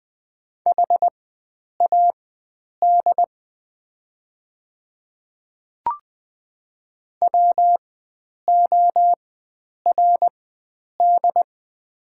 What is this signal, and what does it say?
Morse code, heard as a keyed tone, about 20 wpm, spaced as for 9 wpm, 700 Hz. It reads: HAD WORD
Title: 4x Character Spacing - Top 100 Words - 20wpm